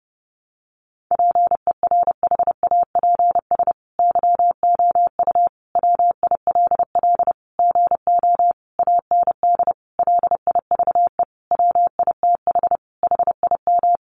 Morse code, heard as a keyed tone, about 30 wpm, 700 Hz. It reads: PERHAPS YOU WILL GO AND LIVE WITH HIM
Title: Sentences from Top 500 Words 30wpm